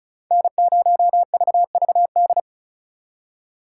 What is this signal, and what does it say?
Morse code, heard as a keyed tone, about 47 wpm, 700 Hz. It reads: N0VVD